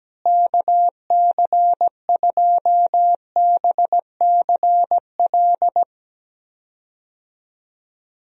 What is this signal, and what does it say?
Morse code, heard as a keyed tone, about 17 wpm, 700 Hz. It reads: KC2BCL
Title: USA Call Signs - 6 Characters - Repeated 2x - 17wpm